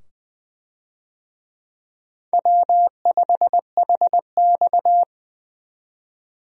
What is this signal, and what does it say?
Morse code, heard as a keyed tone, about 20 wpm, 700 Hz. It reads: W5HX
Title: USA Call Signs - 4 Characters - 20wpm